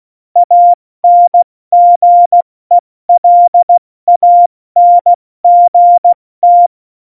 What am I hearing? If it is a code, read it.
Morse code, heard as a keyed tone, about 16 wpm, spaced as for 14 wpm, 700 Hz. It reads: ANGELANGT
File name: Morsekurs / Witze / Witz 13. Oktober 2020, 70 F80